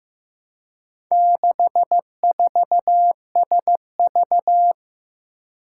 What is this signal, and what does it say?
Morse code, heard as a keyed tone, about 15 wpm, 700 Hz. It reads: 64SV
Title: Four Challenging Letter-Number Combinations 15wpm